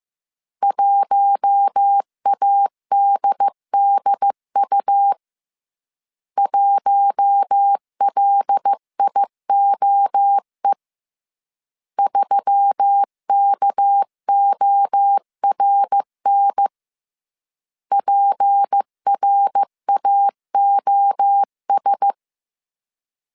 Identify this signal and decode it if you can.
Morse code, heard as a keyed tone, about 15 wpm, 800 Hz. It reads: 1ADDU 1LIOE 3KORN PRAOS